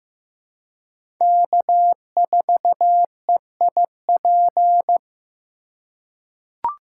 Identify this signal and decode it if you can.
Morse code, heard as a keyed tone, about 15 wpm, 700 Hz. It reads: K4EIP
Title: USA Call Signs - 2x Character Spacing - 15wpm